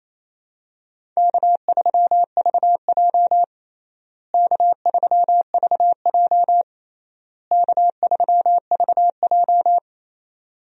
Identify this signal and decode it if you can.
Morse code, heard as a keyed tone, about 28 wpm, 700 Hz. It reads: K3VJ K3VJ K3VJ
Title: USA Call Signs - 4 Characters - Repeated 3x - 28wpm